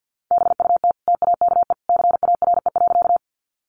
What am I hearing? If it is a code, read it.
Morse code, heard as a keyed tone, about 50 wpm, 700 Hz. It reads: BUT TAKE CARE.
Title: Sentences from Top 300 Words 50wpm